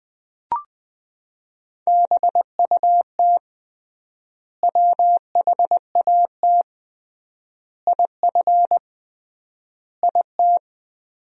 Morse code, heard as a keyed tone, about 20 wpm, 700 Hz. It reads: BUT WHAT IF IT
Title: 3x Word-Spacing - Sentences from Top 100 Words - 20wpm